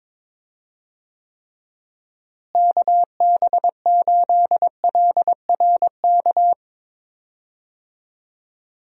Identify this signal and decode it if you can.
Morse code, heard as a keyed tone, about 22 wpm, 700 Hz. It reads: KB8LRK
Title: USA Call Signs - 6 Characters - Repeated 3x - 22wpm